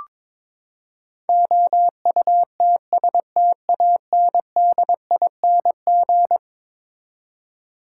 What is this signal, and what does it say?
Morse code, heard as a keyed tone, about 22 wpm, 700 Hz. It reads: OUTSTANDING
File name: practice / 22wpm / Eleven Letter Words - 22wpm